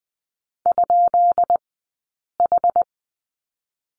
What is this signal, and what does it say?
Morse code, heard as a keyed tone, about 20 wpm, spaced as for 8 wpm, 700 Hz. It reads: ?H
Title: 151 Lesson 25d - Letter p - Two Characters 20wpm